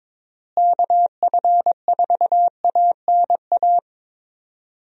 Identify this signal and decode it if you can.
Morse code, heard as a keyed tone, about 22 wpm, 700 Hz. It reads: KF4ANA
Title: USA Call Signs - 6 Characters - Repeated 3x - 22wpm